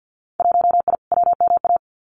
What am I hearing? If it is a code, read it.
Morse code, heard as a keyed tone, about 50 wpm, 700 Hz. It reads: 1I WMA
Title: Winter Field Day Exchanges Encore 50wpm